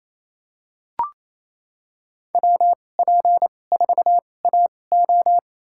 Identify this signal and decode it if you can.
Morse code, heard as a keyed tone, about 28 wpm, spaced as for 20 wpm, 700 Hz. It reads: WP4AO
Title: USA Call Signs - 5 Characters - 2x Character Spacing - 28wpm